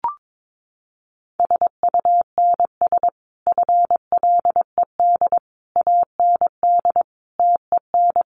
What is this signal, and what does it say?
Morse code, heard as a keyed tone, about 22 wpm, 700 Hz. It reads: SUNS FLED AND TEN
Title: A Sound Of Thunder 22wpm